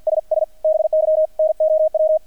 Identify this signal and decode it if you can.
Morse code, heard as a keyed tone, about 48 wpm, 630 Hz, with radio noise. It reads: SS BY NOW
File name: archived / CRSnet